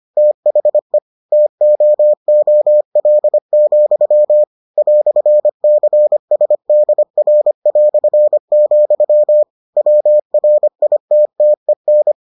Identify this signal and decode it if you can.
Morse code, heard as a keyed tone, about 25 wpm, 600 Hz. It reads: THE TOOL, "CSDR", WRITTEN